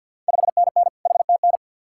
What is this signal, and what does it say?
Morse code, heard as a keyed tone, about 52 wpm, 700 Hz. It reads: 5NN HTN
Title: Ontario QSO Party Exchange 50wpm